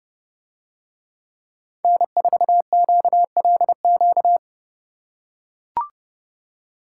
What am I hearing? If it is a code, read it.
Morse code, heard as a keyed tone, about 30 wpm, 700 Hz. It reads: N4QLQ